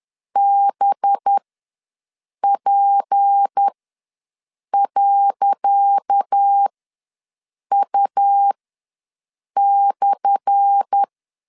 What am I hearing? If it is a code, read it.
Morse code, heard as a keyed tone, about 10 wpm, spaced as for 6 wpm, 800 Hz. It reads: BP.U/
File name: LEZIONI / 10 SIGNS / 10C DOT ws 10wpm 4213